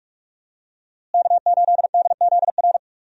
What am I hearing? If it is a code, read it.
Morse code, heard as a keyed tone, about 47 wpm, 700 Hz. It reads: K8DZR
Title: USA Call Signs - 5 Characters - Repeated 2x - 45wpm